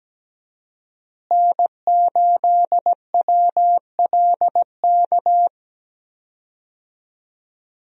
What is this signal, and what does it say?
Morse code, heard as a keyed tone, about 17 wpm, 700 Hz. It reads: N8WLK
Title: USA Call Signs - 5 Characters - Repeated 3x - 17wpm